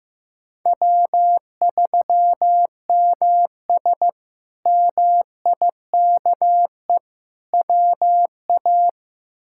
Morse code, heard as a keyed tone, about 15 wpm, 700 Hz. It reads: W3MS MIKE WA
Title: K1USN SST QSO Practice - 15wpm